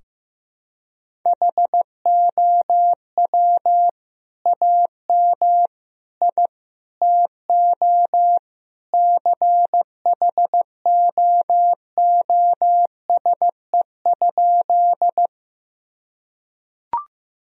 Morse code, heard as a keyed tone, about 15 wpm, 700 Hz. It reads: HOW AM I TO CHOOSE?
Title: Sentences from Top 1000 Words - Review Part 4 - 15wpm